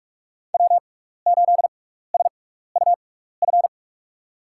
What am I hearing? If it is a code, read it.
Morse code, heard as a keyed tone, about 42 wpm, spaced as for 15 wpm, 700 Hz. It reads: W8SUF
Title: USA Call Signs - 6x Character Spacing - 45wpm